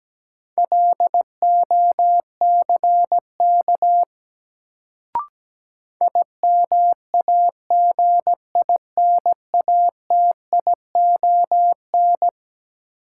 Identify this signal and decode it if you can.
Morse code, heard as a keyed tone, about 17 wpm, 700 Hz. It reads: LOCK IMAGINATION
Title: Sentences from Top 1300 Words - New Word Drill Alt - 17wpm